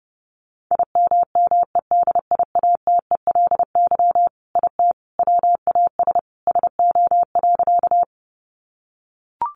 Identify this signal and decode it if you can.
Morse code, heard as a keyed tone, about 30 wpm, 700 Hz. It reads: IMMEDIATELY IT WAS SO.
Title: Sentences from Top 1100 Words 30wpm